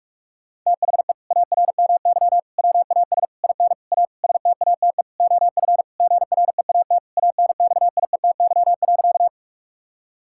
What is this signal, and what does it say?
Morse code, heard as a keyed tone, about 42 wpm, 700 Hz. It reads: THE ARMY WAS IN A STATE OF GREAT ANXIETY.
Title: Sentences from Top 1600 Words 45wpm